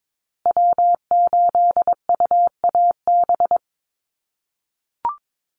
Morse code, heard as a keyed tone, about 22 wpm, 700 Hz. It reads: W8UAB
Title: USA Call Signs Repeated 2x 22wpm